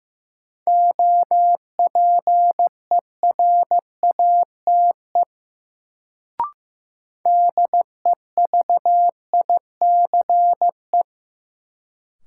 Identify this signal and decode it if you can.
Morse code, heard as a keyed tone, about 15 wpm, 700 Hz. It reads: OPERATE DEVICE
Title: Sentences from Top 1500 Words - New Word Drill - 15wpm